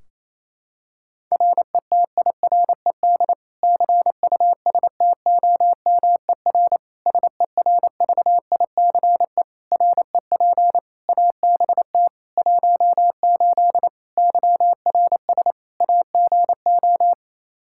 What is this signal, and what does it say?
Morse code, heard as a keyed tone, about 28 wpm, 700 Hz. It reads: RETIRED CUSTOMER SERVICE REP ABT 18 YRS AGO
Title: Practice QSO - Exchange 3 - More Info - 28wpm